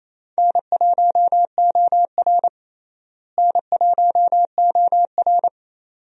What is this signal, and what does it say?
Morse code, heard as a keyed tone, about 28 wpm, 700 Hz. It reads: N1OR N1OR
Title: USA Call Signs - 4 Characters - Repeated 2x - 28wpm